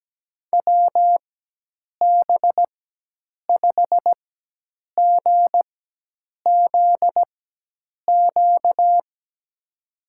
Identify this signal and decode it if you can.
Morse code, heard as a keyed tone, about 17 wpm, spaced as for 8 wpm, 700 Hz. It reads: WB5GZQ
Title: USA Call Signs - 6 Characters - 4x Character Spacing - 17wpm